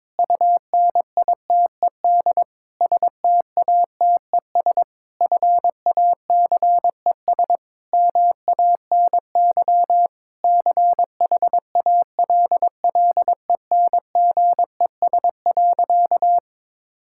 Morse code, heard as a keyed tone, about 22 wpm, 700 Hz. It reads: UNITED STATES FACES MANY CHALLENGES.
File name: Sentences with 7-Word Phrases - 22wpm